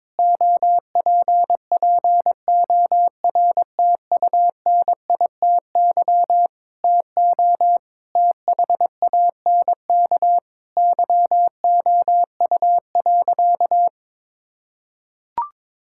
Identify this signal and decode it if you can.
Morse code, heard as a keyed tone, about 22 wpm, 700 Hz. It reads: OPPORTUNITY TO THANK YOU.
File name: practice / 22wpm / Sentences with 8-Word Phrases - 22wpm